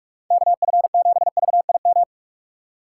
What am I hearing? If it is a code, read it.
Morse code, heard as a keyed tone, about 46 wpm, 700 Hz. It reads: KF7VIM
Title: USA Call Signs - 6 Characters - 2x Character Spacing - 45wpm